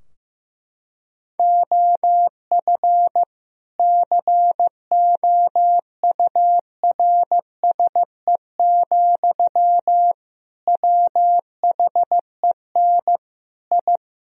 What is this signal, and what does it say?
Morse code, heard as a keyed tone, about 15 wpm, 700 Hz. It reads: OF COURSE, WHEN I
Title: Sentences from Top 1000 Words - Review Part 3 - 15wpm